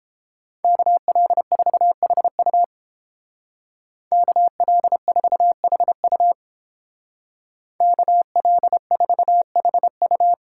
Morse code, heard as a keyed tone, about 31 wpm, 700 Hz. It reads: KL4HU KL4HU KL4HU